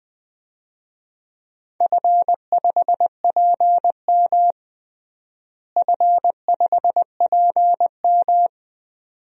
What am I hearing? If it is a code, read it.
Morse code, heard as a keyed tone, about 20 wpm, 700 Hz. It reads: F5PM F5PM
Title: DXCC-100 Call Signs - 4 Characters - Repeated 3x - 20wpm